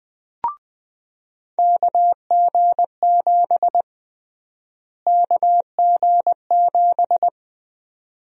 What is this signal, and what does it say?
Morse code, heard as a keyed tone, about 20 wpm, 700 Hz. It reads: KG7 KG7
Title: USA Call Sign Prefixes - Repeated 2x - 20wpm